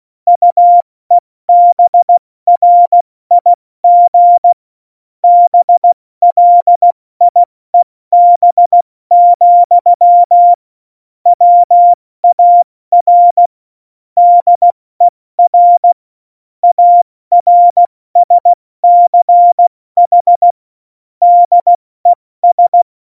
Morse code, heard as a keyed tone, about 16 wpm, 700 Hz. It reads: UEBRIG BLIEB, WAR DER ARSCH DES
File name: Morsekurs / Witze / Witz 01. Dezember 2020, 70 F80